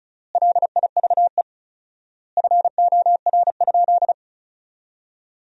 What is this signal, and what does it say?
Morse code, heard as a keyed tone, about 36 wpm, 700 Hz. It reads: LIVE FOR?
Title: 4x Word-Spacing - Sentences from Top 200 Words - 35wpm